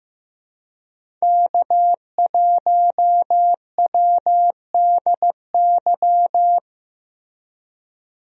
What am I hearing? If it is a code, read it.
Morse code, heard as a keyed tone, about 15 wpm, 700 Hz. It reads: K1WDY
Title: USA Call Signs - 5 Characters - Repeated 2x - 15wpm